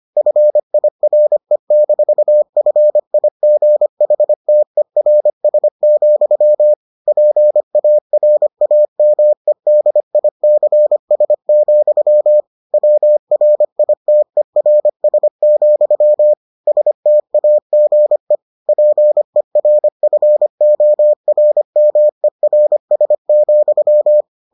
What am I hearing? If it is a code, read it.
Morse code, heard as a keyed tone, about 25 wpm, 600 Hz. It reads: FIRE-FIGHTERS, PARAMEDICS, WRITERS, STAGE PERFORMERS,